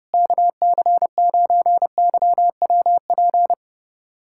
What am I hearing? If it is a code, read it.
Morse code, heard as a keyed tone, about 30 wpm, 700 Hz. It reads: KC9YWP
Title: USA Callsign DE Callsign 30wpm